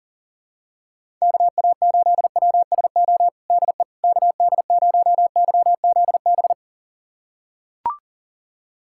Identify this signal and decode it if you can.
Morse code, heard as a keyed tone, about 40 wpm, 700 Hz. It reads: KA8WSO DE KD0YZB
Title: USA Callsign DE Callsign - 6 Characters - 40wpm